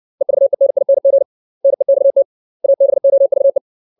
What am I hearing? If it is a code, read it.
Morse code, heard as a keyed tone, about 60 wpm, 550 Hz. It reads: EVENING NEXT ABOVE
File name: 632 words / 632 Words 60WPM